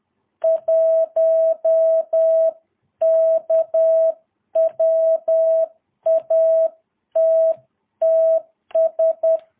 Morse code, heard as a keyed tone, about 10 wpm, 650 Hz. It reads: 1K WATTS
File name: jh8cbh / 2022cw